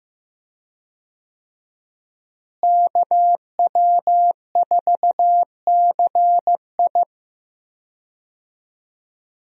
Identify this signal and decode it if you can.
Morse code, heard as a keyed tone, about 15 wpm, 700 Hz. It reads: KW4CI
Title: USA Call Signs - 5 Characters - Repeated 2x - 15wpm